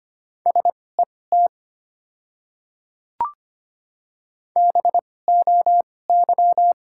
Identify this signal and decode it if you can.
Morse code, heard as a keyed tone, about 25 wpm, spaced as for 18 wpm, 700 Hz. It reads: SET BOY
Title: 2x Character Spacing - Top 200 Words - 25wpm